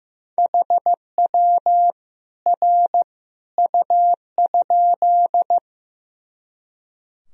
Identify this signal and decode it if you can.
Morse code, heard as a keyed tone, about 15 wpm, 700 Hz. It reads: HW R U?
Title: Sentences from Top 100 Words and Abbreviations 15wpm